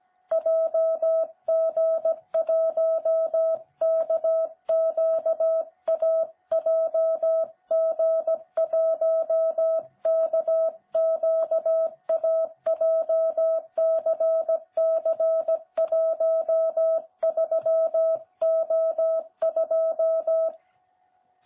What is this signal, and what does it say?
Morse code, heard as a keyed tone, about 16 wpm, 660 Hz. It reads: JG1KQA JG1KQA JCC13O2
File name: jh8cbh / 2021cw